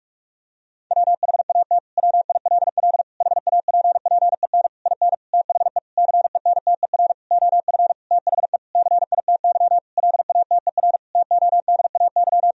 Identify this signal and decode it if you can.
Morse code, heard as a keyed tone, about 46 wpm, 700 Hz. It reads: WHAT WILL HAPPEN IN THE CENTER OF THE CITY LATER TODAY
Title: Sentences from Top 1000 Words - Review Part 2 - 45wpm